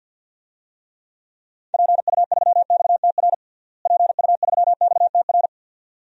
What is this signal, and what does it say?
Morse code, heard as a keyed tone, about 48 wpm, 700 Hz. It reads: PU3XTR PU3XTR
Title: DXCC-100 Call Signs - 6 Characters - Repeated 2x - 50wpm